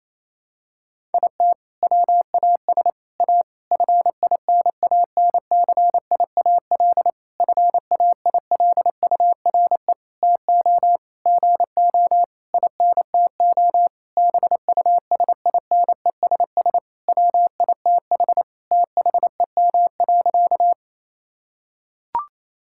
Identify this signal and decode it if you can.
Morse code, heard as a keyed tone, about 28 wpm, 700 Hz. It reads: IT WAS A FINANCIAL FAILURE TO GO INTO BUSINESS WITH THEM.
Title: Sentences from Top 1000 Words - Review Part 6 - 28wpm